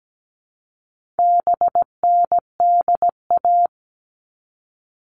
Morse code, heard as a keyed tone, about 17 wpm, 700 Hz. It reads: BNDA
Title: All Cut Numbers - 4-Digit 17wpm